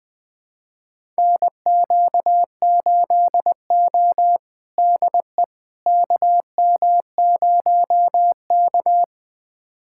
Morse code, heard as a keyed tone, about 20 wpm, 700 Hz. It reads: NQ8O DE KM0K
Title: USA Callsign DE Callsign - 4 Characters - 20wpm